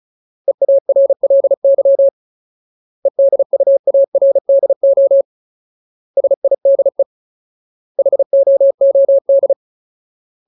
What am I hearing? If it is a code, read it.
Morse code, heard as a keyed tone, about 36 wpm, 550 Hz. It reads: EARLY EDUARDO SIDE HOOD
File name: Ultimate Word List / 35WPM Ultimate